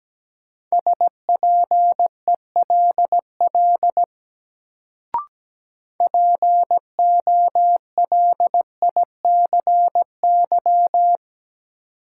Morse code, heard as a keyed tone, about 17 wpm, 700 Hz. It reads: SPELL POLICY
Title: Sentences from Top 1200 Words - New Word Drill Alt - 17wpm